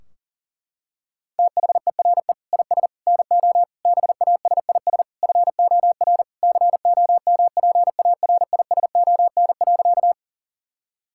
Morse code, heard as a keyed tone, about 37 wpm, 700 Hz. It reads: THERE IS NO BASIS FOR COMPARISON.
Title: Sentences from Top 1600 Words 40wpm